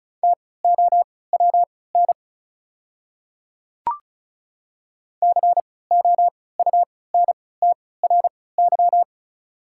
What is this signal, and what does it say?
Morse code, heard as a keyed tone, about 36 wpm, spaced as for 20 wpm, 690 Hz. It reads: TOWN COUNTRY
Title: 3x Character Spacing - Top 500 Words - 35wpm